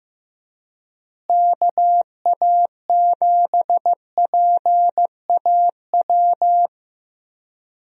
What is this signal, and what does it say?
Morse code, heard as a keyed tone, about 15 wpm, 700 Hz. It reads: KA7PAW
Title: USA Call Signs - 6 Characters - Repeated 3x - 15wpm